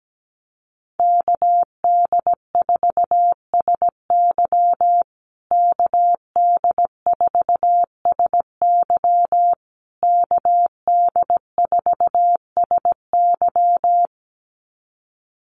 Morse code, heard as a keyed tone, about 17 wpm, 700 Hz. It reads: KD4SY KD4SY KD4SY
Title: USA Call Signs Repeated 3x 17wpm